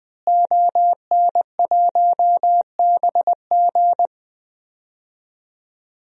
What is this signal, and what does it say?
Morse code, heard as a keyed tone, about 21 wpm, 700 Hz. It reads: ON1BG
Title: DXCC-100 Call Signs - 5 Characters - Repeated 3x - 20wpm